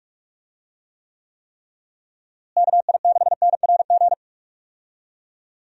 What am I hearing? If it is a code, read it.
Morse code, heard as a keyed tone, about 46 wpm, 700 Hz. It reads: KI6NRG
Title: USA Call Signs - 6 Characters - Repeated 3x - 45wpm